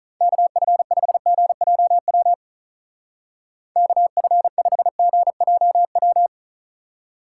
Morse code, heard as a keyed tone, about 42 wpm, 700 Hz. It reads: KF5GJW KF5GJW
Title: Speed Racing - USA 2x3 Call Signs 35wpm